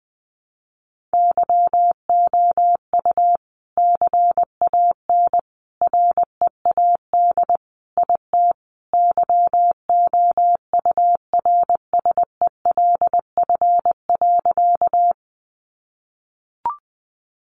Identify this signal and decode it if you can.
Morse code, heard as a keyed tone, about 19 wpm, 700 Hz. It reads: YOU CAN READ IT YOURSELF.
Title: Sentences from Top 600 Words 20wpm